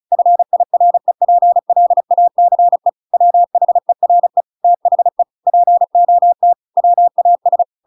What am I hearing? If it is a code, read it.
Morse code, heard as a keyed tone, about 34 wpm, 700 Hz. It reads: FIREPLACE WHERE THE POT WAS
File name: Pinocchio / Pinocchio ch5 35 35 ik1hge